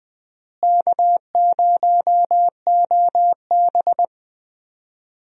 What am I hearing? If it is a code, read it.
Morse code, heard as a keyed tone, about 21 wpm, 700 Hz. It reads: K0OB
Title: USA Call Signs - 4 Characters - Repeated 3x - 20wpm